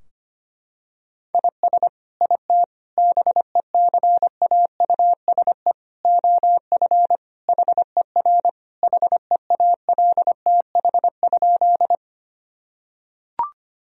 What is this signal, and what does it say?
Morse code, heard as a keyed tone, about 25 wpm, 700 Hz. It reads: IS IT BECAUSE OF HER HEALTH?
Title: Sentences from Top 1000 Words - Review Part 3 - 25wpm